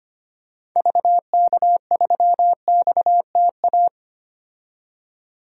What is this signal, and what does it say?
Morse code, heard as a keyed tone, about 25 wpm, 690 Hz. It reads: VK3XTA
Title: DXCC-100 Call Signs - 6 Characters - 25wpm